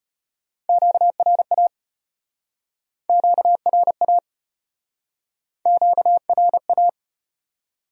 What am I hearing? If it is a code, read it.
Morse code, heard as a keyed tone, about 35 wpm, 700 Hz. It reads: QRA QRA QRA